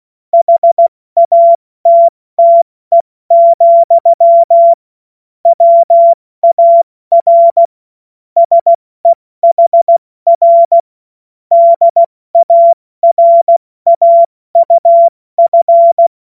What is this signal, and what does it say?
Morse code, heard as a keyed tone, about 16 wpm, 680 Hz. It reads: HATTE, WAR SEHR DARAUF
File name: Morsekurs / Witze / Witz 10. Mai 2022 70 F80